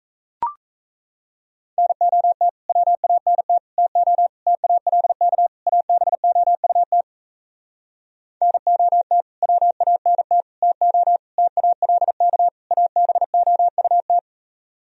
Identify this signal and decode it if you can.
Morse code, heard as a keyed tone, about 37 wpm, 700 Hz. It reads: NOT WANT TO TALK ABOUT NOT WANT TO TALK ABOUT